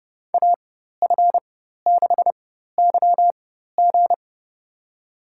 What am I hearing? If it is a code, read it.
Morse code, heard as a keyed tone, about 32 wpm, spaced as for 14 wpm, 700 Hz. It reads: AF6YG
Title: USA Call Signs - 4x Character Spacing - 30wpm